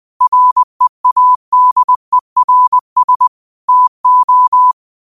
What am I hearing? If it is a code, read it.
Morse code, heard as a keyed tone, about 20 wpm, 1000 Hz. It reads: READERS TO
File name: 20wpm